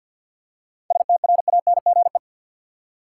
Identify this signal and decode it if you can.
Morse code, heard as a keyed tone, about 46 wpm, 700 Hz. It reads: STRANGE
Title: Seven Letter Words - 50wpm